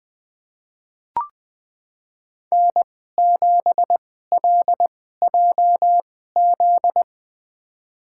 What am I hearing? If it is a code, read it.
Morse code, heard as a keyed tone, about 21 wpm, spaced as for 15 wpm, 700 Hz. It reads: N7LJZ
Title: USA Call Signs - 5 Characters - 2x Character Spacing - 20wpm